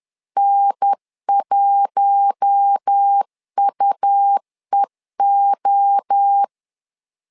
Morse code, heard as a keyed tone, about 10 wpm, 800 Hz. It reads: N1UEO
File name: LEZIONI / 4 Q1 10wpm 2448